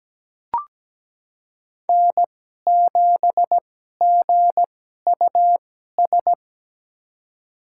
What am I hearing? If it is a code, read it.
Morse code, heard as a keyed tone, about 17 wpm, spaced as for 12 wpm, 700 Hz. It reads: N7GUS
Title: USA Call Signs - 5 Characters - 2x Character Spacing - 17wpm